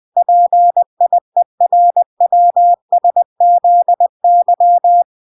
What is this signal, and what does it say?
Morse code, heard as a keyed tone, about 20 wpm, 700 Hz. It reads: PIERWSZY